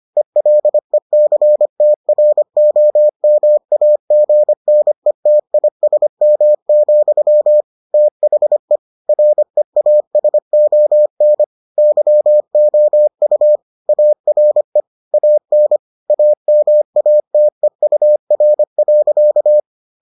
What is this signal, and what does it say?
Morse code, heard as a keyed tone, about 25 wpm, 600 Hz. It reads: ELECTROMAGNETISM, THE REASON YOU ARE AN AMATEUR.